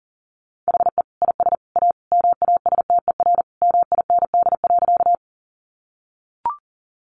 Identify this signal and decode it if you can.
Morse code, heard as a keyed tone, about 40 wpm, 700 Hz. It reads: HE IS A MASTER MIND.
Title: Sentences from Top 1300 Words 40wpm